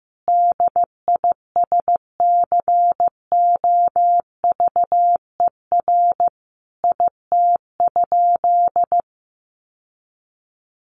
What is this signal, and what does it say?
Morse code, heard as a keyed tone, about 14 wpm, 700 Hz. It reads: DISCOVER IT?
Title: Sentences from Top 1100 Words 15wpm